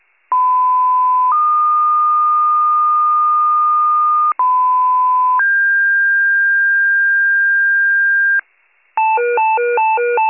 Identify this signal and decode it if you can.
Two-tone page, heard as a one-second tone, then a three-second tone, about 1010 then 1220 Hz, about 1010 then 1600 Hz, with noise. It is heard from 0.3 to 8.4 s.